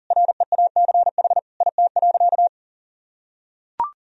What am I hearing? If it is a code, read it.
Morse code, heard as a keyed tone, about 40 wpm, 700 Hz. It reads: REACH IT.
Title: Sentences from Top 400 Words and Abbreviations 40wpm